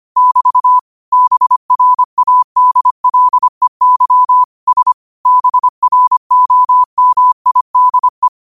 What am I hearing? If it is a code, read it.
Morse code, heard as a keyed tone, about 25 wpm, 1000 Hz. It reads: <BT> BRADLEY S BROMIDE
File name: qotdfiles / 25wpm